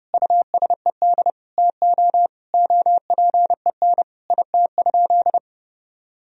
Morse code, heard as a keyed tone, about 30 wpm, 700 Hz. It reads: USED TO OPEN IT?
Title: Sentences from Top 1000 Words - Review Part 4 - 30wpm